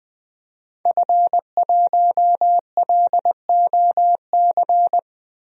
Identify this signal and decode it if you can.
Morse code, heard as a keyed tone, about 21 wpm, 700 Hz. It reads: F1LOC
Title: DXCC-100 Call Signs - 5 Characters - Repeated 3x - 20wpm